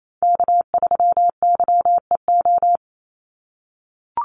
Morse code, heard as a keyed tone, about 28 wpm, 700 Hz. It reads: K3YEO
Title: Call Signs - 28wpm